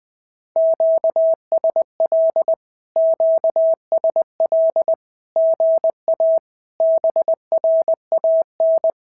Morse code, heard as a keyed tone, about 20 wpm, 690 Hz. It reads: QSL QSL GA BRAN
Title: SOTA QSO Practice 20wpm